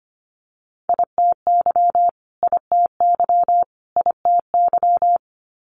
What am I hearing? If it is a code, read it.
Morse code, heard as a keyed tone, about 25 wpm, 700 Hz. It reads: ITY ITY ITY
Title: Multi-letter Phonemes Repeated 3x 25wpm